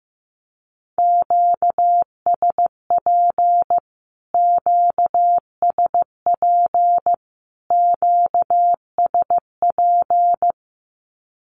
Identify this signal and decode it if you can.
Morse code, heard as a keyed tone, about 15 wpm, 700 Hz. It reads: QSP QSP QSP
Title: Q-Codes Repeated 3x 15wpm